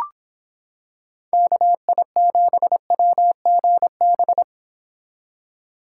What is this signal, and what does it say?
Morse code, heard as a keyed tone, about 26 wpm, 700 Hz. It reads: KI7WGB